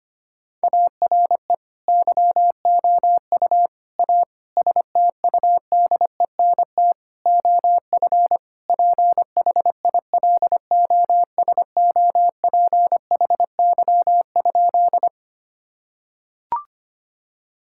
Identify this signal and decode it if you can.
Morse code, heard as a keyed tone, about 25 wpm, 700 Hz. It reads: ARE YOU A STUDENT OF PHILOSOPHY?
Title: Sentences from Top 1000 Words - Review Part 6 - 25wpm